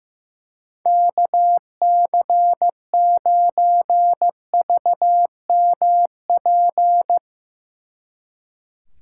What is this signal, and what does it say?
Morse code, heard as a keyed tone, about 16 wpm, 700 Hz. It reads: KC9VMP
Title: Speed Racing - USA 2x3 Call Signs 15wpm